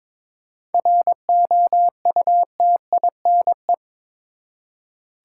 Morse code, heard as a keyed tone, about 22 wpm, 700 Hz. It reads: ROUTINE
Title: Seven Letter Words - 22wpm